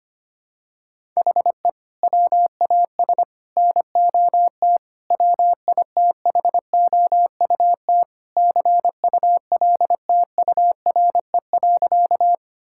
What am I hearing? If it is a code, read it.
Morse code, heard as a keyed tone, about 25 wpm, 700 Hz. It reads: HE WAS NOT WITHOUT CULTURE.
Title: Sentences from Top 1000 Words - Review Part 6 - 25wpm